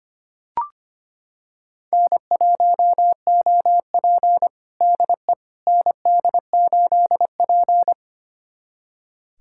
Morse code, heard as a keyed tone, about 25 wpm, 700 Hz. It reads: N1OP DE ND8P
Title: USA Callsign DE Callsign - 4 Characters - 25wpm